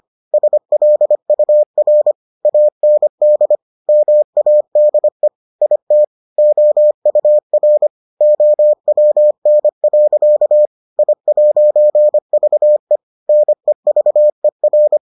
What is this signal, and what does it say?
Morse code, heard as a keyed tone, about 25 wpm, 600 Hz. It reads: SLUR AND MADE IT OUR OWN. I'VE NEVER